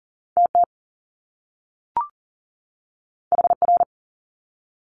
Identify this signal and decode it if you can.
Morse code, heard as a keyed tone, about 40 wpm, 700 Hz. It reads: TT HR
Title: Top Two Letter Combinations in Words 40wpm